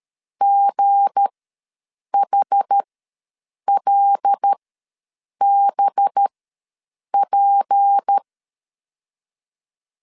Morse code, heard as a keyed tone, about 13 wpm, spaced as for 7 wpm, 800 Hz. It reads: GHLBP
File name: LEZIONI / 7 GF9Z / 7 GF9 ws 12wpm 3344